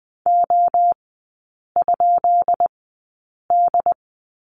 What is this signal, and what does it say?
Morse code, heard as a keyed tone, about 20 wpm, spaced as for 8 wpm, 700 Hz. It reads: O?D
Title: 152 Lesson 25e - Letter p - Three Characters 20wpm